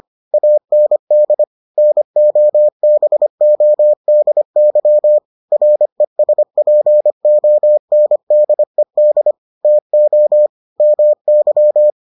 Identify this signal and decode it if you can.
Morse code, heard as a keyed tone, about 25 wpm, 600 Hz. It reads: AND NOBODY RESPONDED TO MY